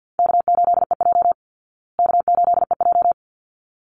Morse code, heard as a keyed tone, about 48 wpm, 700 Hz. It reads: K8EJ K8EJ
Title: Call Signs Repeated 3x - 50wpm